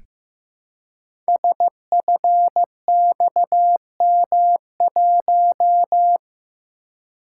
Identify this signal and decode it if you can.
Morse code, heard as a keyed tone, about 15 wpm, 700 Hz. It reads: SFXM1
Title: Code Groups - 5 Letter:Numbers - 15wpm